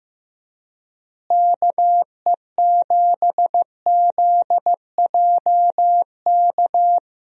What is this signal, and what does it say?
Morse code, heard as a keyed tone, about 15 wpm, 700 Hz. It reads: KE7ZJK
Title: USA Call Signs - 6 Characters - Repeated 2x - 15wpm